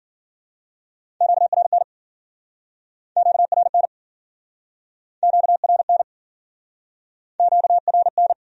Morse code, heard as a keyed tone, about 55 wpm, 700 Hz. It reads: QRN QRN QRN QRN